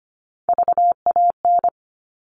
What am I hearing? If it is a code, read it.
Morse code, heard as a keyed tone, about 25 wpm, 700 Hz. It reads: VAN
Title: All Cut Numbers - 3-Digit 25wpm